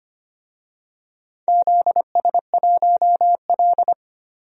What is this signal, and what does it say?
Morse code, heard as a keyed tone, about 25 wpm, 700 Hz. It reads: ZS1L
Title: DXCC-100 Call Signs - 4 Characters - Repeated 2x - 25wpm